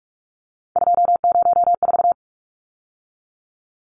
Morse code, heard as a keyed tone, about 44 wpm, 700 Hz. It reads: 204
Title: Numbers - 3-Digit 45wpm